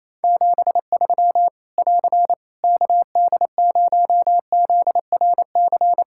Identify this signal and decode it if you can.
Morse code, heard as a keyed tone, about 28 wpm, 700 Hz. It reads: 73 <AR> KD0ZRC
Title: K1USN SST QSO Practice 28wpm